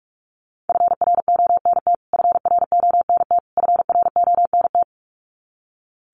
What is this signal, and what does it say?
Morse code, heard as a keyed tone, about 44 wpm, 700 Hz. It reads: FRONT FRONT FRONT
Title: Top 500 Words Repeated 45wpm